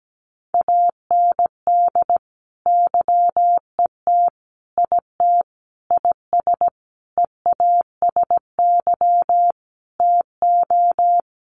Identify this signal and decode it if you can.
Morse code, heard as a keyed tone, about 17 wpm, 700 Hz. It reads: AND YET IT IS EASY TO
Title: Sentences from Top 1400 Words 17wpm